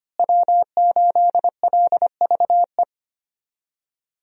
Morse code, heard as a keyed tone, about 25 wpm, 700 Hz. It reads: W8LVE
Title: USA Call Signs - 5 Characters - Repeated 3x - 25wpm